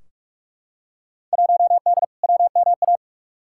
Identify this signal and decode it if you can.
Morse code, heard as a keyed tone, about 43 wpm, 700 Hz. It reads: 1D WMA
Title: Field Day Exchanges - 45wpm